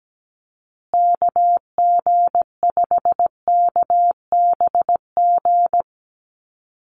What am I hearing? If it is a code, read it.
Morse code, heard as a keyed tone, about 17 wpm, 700 Hz. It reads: KG5KBG
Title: USA Call Signs Repeated 3x 17wpm